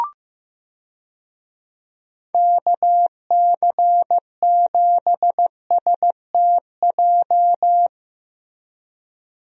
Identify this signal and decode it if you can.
Morse code, heard as a keyed tone, about 15 wpm, 700 Hz. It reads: KC7STJ
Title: USA Call Signs - 6 Characters - 15wpm